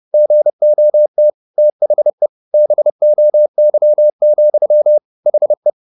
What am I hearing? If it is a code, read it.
Morse code, heard as a keyed tone, about 30 wpm, 600 Hz. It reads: GOT THE BOY, HE
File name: the old man and the sea / the old man and the sea 1 30wpm